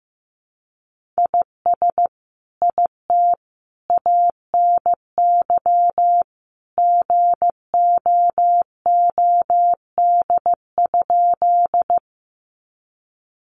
Morse code, heard as a keyed tone, about 15 wpm, 700 Hz. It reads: IS IT ANY GOOD?
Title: Sentences from Top 200 Words 15wpm